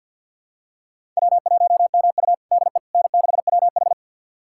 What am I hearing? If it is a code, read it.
Morse code, heard as a keyed tone, about 52 wpm, 700 Hz. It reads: W1MU DE N6WH
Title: USA Callsign DE Callsign - 4 Characters - 50wpm